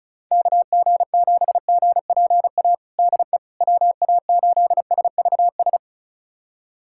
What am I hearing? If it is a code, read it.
Morse code, heard as a keyed tone, about 36 wpm, 700 Hz. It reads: KG7GPA DE WA8SVS
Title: USA Callsign DE Callsign 35wpm